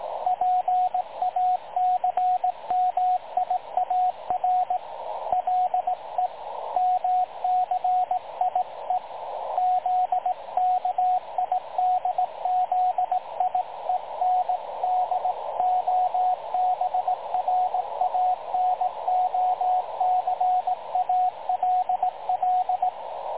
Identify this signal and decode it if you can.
Morse code, heard as a keyed tone, about 18 wpm, 730 Hz, with radio noise. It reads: PACMIAR LEMCIE ZKIDZIEN DOBRANOC ALL
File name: dlazaawansowanych / nagraniazpasma / websdr recording 3565.0kHz